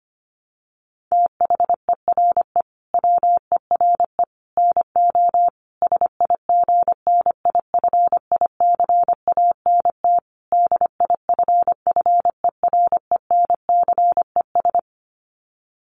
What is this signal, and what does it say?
Morse code, heard as a keyed tone, about 25 wpm, 700 Hz. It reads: THERE WERE NO SIGNIFICANT DIFFERENCES
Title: Sets of 5 Words Encore2 25wpm